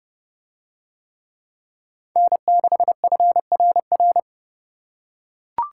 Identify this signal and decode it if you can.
Morse code, heard as a keyed tone, about 30 wpm, 700 Hz. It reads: N6FRR